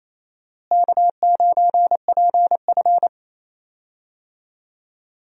Morse code, heard as a keyed tone, about 28 wpm, 700 Hz. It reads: K9PF
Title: USA Call Signs - 4 Characters - Repeated 3x - 28wpm